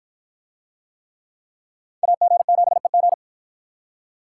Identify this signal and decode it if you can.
Morse code, heard as a keyed tone, about 47 wpm, 700 Hz. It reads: AG7EG